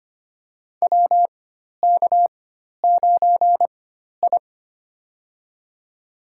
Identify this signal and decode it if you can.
Morse code, heard as a keyed tone, about 25 wpm, spaced as for 12 wpm, 700 Hz. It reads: WK9I
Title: USA Call Signs - 4x Character Spacing - 25wpm